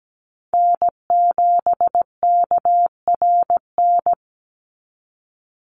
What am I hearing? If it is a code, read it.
Morse code, heard as a keyed tone, about 17 wpm, 700 Hz. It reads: N7KRN
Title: NA SOTA Chaser Callsigns 17wpm